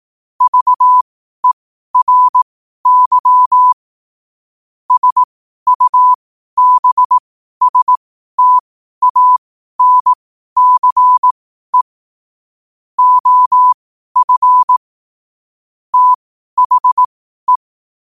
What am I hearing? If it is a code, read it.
Morse code, heard as a keyed tone, about 18 wpm, spaced as for 12 wpm, 1000 Hz. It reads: VERY SUBSTANCE OF THE
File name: qotdfiles / 12wpm